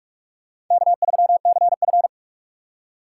Morse code, heard as a keyed tone, about 46 wpm, 700 Hz. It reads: K3CF
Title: USA Call Signs - 4 Characters - 4x Character Spacing - 45wpm